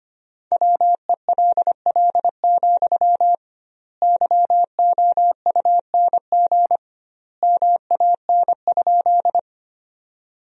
Morse code, heard as a keyed tone, about 25 wpm, 700 Hz. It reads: WELL, YOUNG MAN?
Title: 2x Word-Spacing - Sentences from Top 400 Words - 25wpm